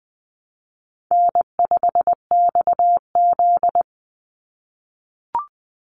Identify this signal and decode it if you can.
Morse code, heard as a keyed tone, about 20 wpm, 700 Hz. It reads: N5XZ
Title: Easy Call Signs - 20wpm